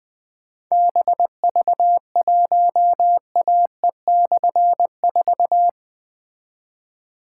Morse code, heard as a keyed tone, about 20 wpm, 700 Hz. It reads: BV1AE/4
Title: Difficult Call Signs 20wpm